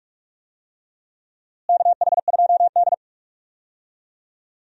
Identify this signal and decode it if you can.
Morse code, heard as a keyed tone, about 42 wpm, 700 Hz. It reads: KH2D
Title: USA Call Signs - 4 Characters - Repeated 2x - 45wpm